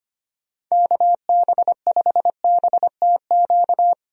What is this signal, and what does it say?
Morse code, heard as a keyed tone, about 25 wpm, 700 Hz. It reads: KB5BTQ
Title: USA Call Signs - 6 Characters - Repeated 3x - 25wpm